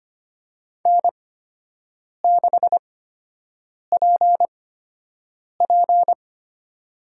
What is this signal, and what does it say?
Morse code, heard as a keyed tone, about 25 wpm, spaced as for 7 wpm, 700 Hz. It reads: N6PP
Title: USA Call Signs - 4 Characters - 8x Character Spacing - 25wpm